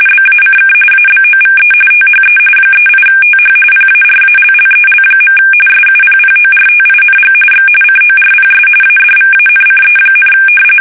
RTTY, about 45.45 baud. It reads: BACK ABOVE THE 38TH PARALLEL. SUPPORTING X CORPS ADVANCE,
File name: fsk850 / battleshipnj